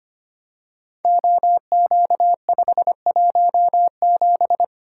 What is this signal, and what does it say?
Morse code, heard as a keyed tone, about 25 wpm, 700 Hz. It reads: OQ517
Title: Code Groups - 5 Letter:Numbers - 25wpm